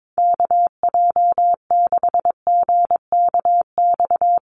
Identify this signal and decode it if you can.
Morse code, heard as a keyed tone, about 22 wpm, 700 Hz. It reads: KJ6GKX
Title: USA Call Signs Repeated 3x 22wpm